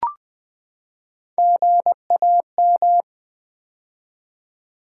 Morse code, heard as a keyed tone, about 20 wpm, 700 Hz. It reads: GAM
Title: Build A Word - 3 Letters At A Time - 8-Letter Words - 20wpm